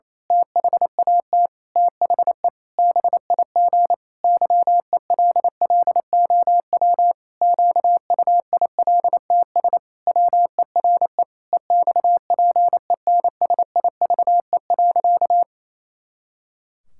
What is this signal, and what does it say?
Morse code, heard as a keyed tone, about 27 wpm, 700 Hz. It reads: THAT THE BIG YELLOW QUILTS WERE EXPENSIVE.
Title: Pangram Challenge v2 - 28wpm